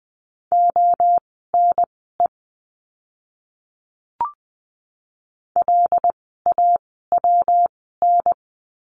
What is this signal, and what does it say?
Morse code, heard as a keyed tone, about 20 wpm, spaced as for 15 wpm, 700 Hz. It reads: ONE LAWN
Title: 115 Lesson 19f - Letter w - Words 20wpm